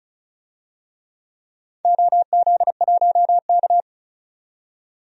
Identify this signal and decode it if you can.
Morse code, heard as a keyed tone, about 36 wpm, 700 Hz. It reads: OZ1K
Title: DXCC-100 Call Signs - 4 Characters - 35wpm